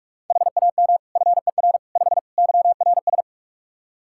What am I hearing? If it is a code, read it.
Morse code, heard as a keyed tone, about 42 wpm, 700 Hz. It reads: HAM FER 5 YRS
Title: Practice QSO - Exchange 3 - More Info - 45wpm